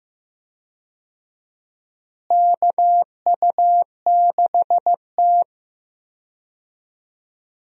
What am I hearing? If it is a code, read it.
Morse code, heard as a keyed tone, about 16 wpm, 700 Hz. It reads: KU6T
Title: USA Call Signs - 4 Characters - Repeated 2x - 15wpm